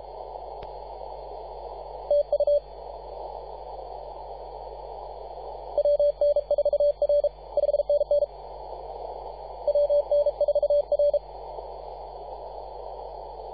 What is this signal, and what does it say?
Morse code, heard as a keyed tone, about 35 wpm, 580 Hz. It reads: TU WN4R 5NN WN4R
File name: posts / 2015-sept-24-2118z-t2gc-18-071-cw